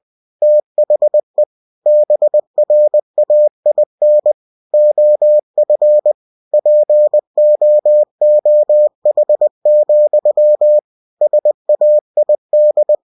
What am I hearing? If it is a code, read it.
Morse code, heard as a keyed tone, about 20 wpm, 600 Hz. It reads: THE BRAIN OF POOH, SAID
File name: cw-books / winnie-the-pooh / winnie-20wpm-page-0100